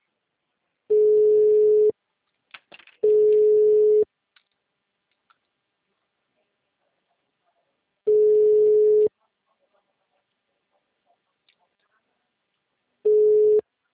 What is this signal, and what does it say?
Ringback tone (Europe, 425 Hz).
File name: out- -2010- - - .46063